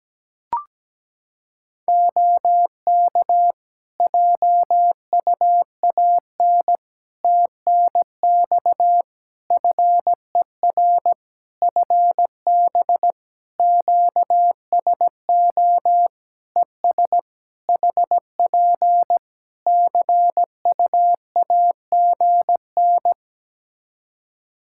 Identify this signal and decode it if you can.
Morse code, heard as a keyed tone, about 17 wpm, 700 Hz. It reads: OK JUAN TNX FER FB QSO ES HP CUAGN
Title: Practice QSO - Exchange 4 - Ending - 17wpm